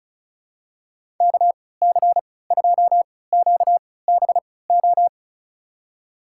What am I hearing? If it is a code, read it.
Morse code, heard as a keyed tone, about 36 wpm, spaced as for 20 wpm, 700 Hz. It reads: KC2QBO
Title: USA Call Signs - 3x Character Spacing - 35wpm